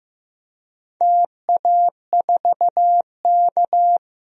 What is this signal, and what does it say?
Morse code, heard as a keyed tone, about 15 wpm, 700 Hz. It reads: TA4K
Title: DXCC-100 Call Signs - 4 Characters - Repeated 2x - 15wpm